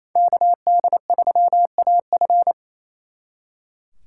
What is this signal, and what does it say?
Morse code, heard as a keyed tone, about 35 wpm, 700 Hz. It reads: KD3AF